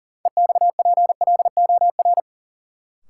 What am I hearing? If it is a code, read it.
Morse code, heard as a keyed tone, about 40 wpm, 700 Hz. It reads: EXPLOR
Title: Sentences from Top 1100 Words - New Word Drill - 40wpm